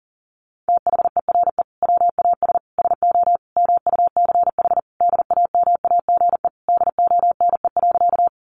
Morse code, heard as a keyed tone, about 40 wpm, 700 Hz. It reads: THERE WAS SO MUCH DAMAGE DONE.
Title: Sentences from Top 1100 Words 40wpm